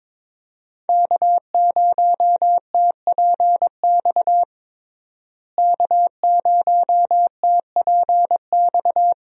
Morse code, heard as a keyed tone, about 21 wpm, 700 Hz. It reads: K0TPX K0TPX
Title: USA Call Signs - 5 Characters - Repeated 2x - 22wpm